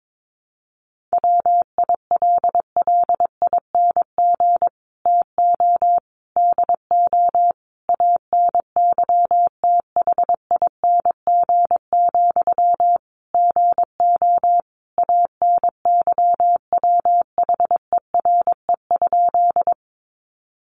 Morse code, heard as a keyed tone, about 22 wpm, 700 Hz. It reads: WILLING TO DO ANYTHING, GO ANYWHERE?
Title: Sentences from Top 1400 Words 22wpm